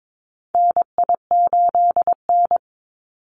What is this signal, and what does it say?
Morse code, heard as a keyed tone, about 22 wpm, 700 Hz. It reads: NI8N
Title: Call Signs Repeated 2x - 22wpm